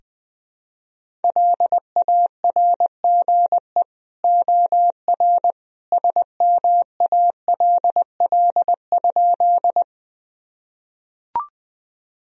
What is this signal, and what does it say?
Morse code, heard as a keyed tone, about 20 wpm, 700 Hz. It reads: LARGE OR SMALL?
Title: Sentences from Top 1000 Words - Review Part 1 - 20wpm